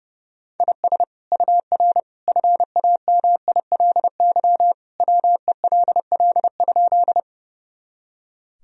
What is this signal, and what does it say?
Morse code, heard as a keyed tone, about 32 wpm, 700 Hz. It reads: IS UR FAMILY WELL?
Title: Sentences from Top 400 Words and Abbreviations 30wpm